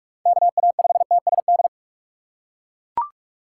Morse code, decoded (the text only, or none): KA5TSD